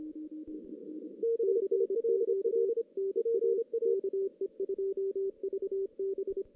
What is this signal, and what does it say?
Morse code, heard as a keyed tone, about 25 wpm, 390 Hz. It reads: CQ NA KE2VB
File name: 2016 JAN CW NAQP / 2016 Jan naqp k5ot - 0-4